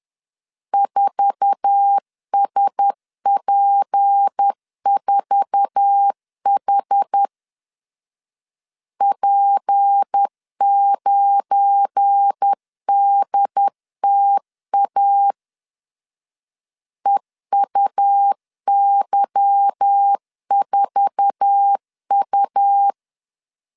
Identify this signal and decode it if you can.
Morse code, heard as a keyed tone, about 10 wpm, 800 Hz. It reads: 4SP4H P9DTA EUY4U
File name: LEZIONI / 8 YJ45 / 8 YJ4 10wpm 2633